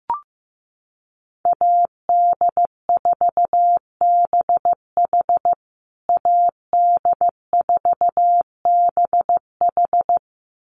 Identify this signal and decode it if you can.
Morse code, heard as a keyed tone, about 15 wpm, 700 Hz. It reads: AD4BH AD4BH
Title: USA Call Signs Repeated 2x 15wpm